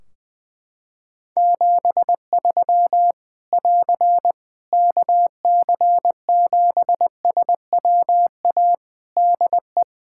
Morse code, heard as a keyed tone, about 20 wpm, 700 Hz. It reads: 73 <AR> KC7SWA DE
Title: Practice QSO - Exchange 4 - Ending - 20wpm